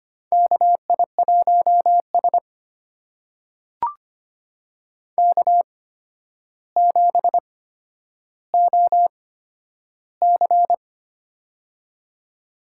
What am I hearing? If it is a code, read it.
Morse code, heard as a keyed tone, about 25 wpm, spaced as for 11 wpm, 700 Hz. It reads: KI1S K7OC
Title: USA Call Signs - 4 Characters - 8x Character Spacing - 25wpm